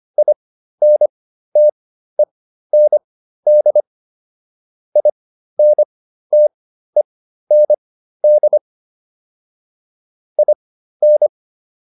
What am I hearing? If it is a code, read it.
Morse code, heard as a keyed tone, about 25 wpm, spaced as for 13 wpm, 600 Hz. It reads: INTEND INTEND IN